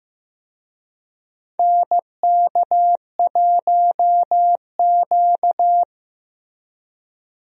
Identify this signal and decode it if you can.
Morse code, heard as a keyed tone, about 15 wpm, 700 Hz. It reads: NK1Q
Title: USA Call Signs - 4 Characters - Repeated 3x - 15wpm